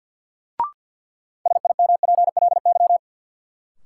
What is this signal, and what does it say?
Morse code, heard as a keyed tone, about 52 wpm, 700 Hz. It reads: SIMPLY
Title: Sentences from Top 600 Words - New Word Drill - 50wpm